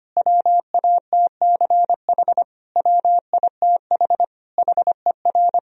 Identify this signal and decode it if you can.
Morse code, heard as a keyed tone, about 25 wpm, 700 Hz. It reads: WATCH WITH HER
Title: Sentences from Top 1000 Words - Review Part 4 - 25wpm